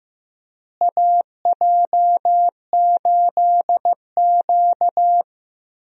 Morse code, heard as a keyed tone, about 15 wpm, 700 Hz. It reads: AJ8Q
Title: USA Call Signs - 4 Characters - Repeated 3x - 15wpm